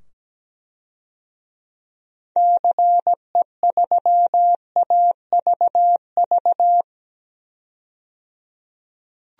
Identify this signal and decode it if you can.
Morse code, heard as a keyed tone, about 17 wpm, 700 Hz. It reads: CE3AVV
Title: DXCC-100 Call Signs - 6 Characters - 17wpm